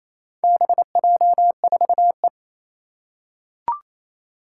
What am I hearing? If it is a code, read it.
Morse code, heard as a keyed tone, about 28 wpm, 700 Hz. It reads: BJ4E
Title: Four Challenging Letter-Number Combinations 28wpm